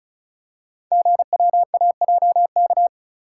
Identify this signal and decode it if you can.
Morse code, heard as a keyed tone, about 36 wpm, 700 Hz. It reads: GWAJK
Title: Code Groups - 5 Letter:Numbers - 35wpm